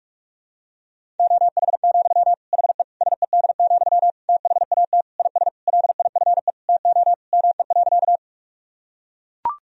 Morse code, heard as a keyed tone, about 41 wpm, 700 Hz. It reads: OH, HE SED, THAT IS LIFE TO ME.
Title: Sentences from Top 500 Words and Abbreviations 45wpm